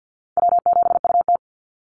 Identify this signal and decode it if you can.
Morse code, heard as a keyed tone, about 50 wpm, 700 Hz. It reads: W7UT
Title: Easy Call Signs - 50wpm